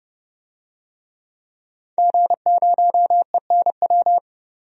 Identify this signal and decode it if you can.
Morse code, heard as a keyed tone, about 31 wpm, 700 Hz. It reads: G0ENW
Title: DXCC-100 Call Signs - 5 Characters - 30wpm